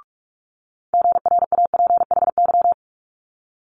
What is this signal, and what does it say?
Morse code, heard as a keyed tone, about 46 wpm, 700 Hz. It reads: GRAPHY
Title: Multi-letter Phonemes Quick Fire 45wpm